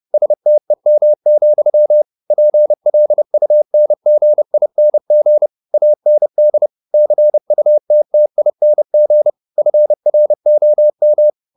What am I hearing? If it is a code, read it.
Morse code, heard as a keyed tone, about 30 wpm, 600 Hz. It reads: STEM, PLUNGING AND CUTTING FROM